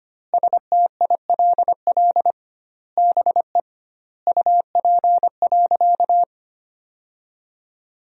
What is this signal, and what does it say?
Morse code, heard as a keyed tone, about 25 wpm, 700 Hz. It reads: STILL BE UP.
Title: 2x Word-Spacing - Sentences from Top 300 Words - 25wpm